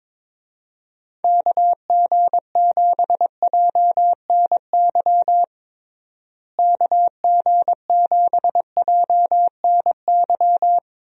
Morse code, heard as a keyed tone, about 22 wpm, 700 Hz. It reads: KG7JNY KG7JNY
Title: USA Call Signs - 6 Characters - Repeated 2x - 22wpm